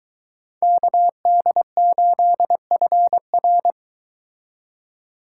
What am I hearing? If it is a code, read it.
Morse code, heard as a keyed tone, about 23 wpm, 700 Hz. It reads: KD8FR